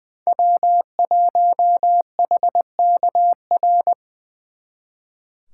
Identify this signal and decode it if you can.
Morse code, heard as a keyed tone, about 24 wpm, 700 Hz. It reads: W1HKR
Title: Speed Racing - USA 1x3 Call Signs 20wpm